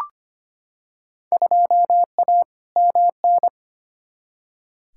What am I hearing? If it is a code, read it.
Morse code, heard as a keyed tone, about 25 wpm, 700 Hz. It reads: 2A MN
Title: Field Day Exchanges Encore - 25wpm